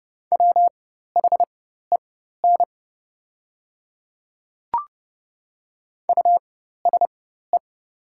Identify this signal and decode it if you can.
Morse code, heard as a keyed tone, about 30 wpm, spaced as for 14 wpm, 700 Hz. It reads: WHEN USE
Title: 4x Character Spacing - Top 100 Words - 30wpm